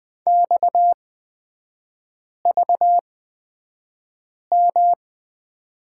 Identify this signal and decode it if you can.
Morse code, heard as a keyed tone, about 20 wpm, spaced as for 5 wpm, 690 Hz. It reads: XVM
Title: 255 Lesson 41e - Period - Three Characters 20wpm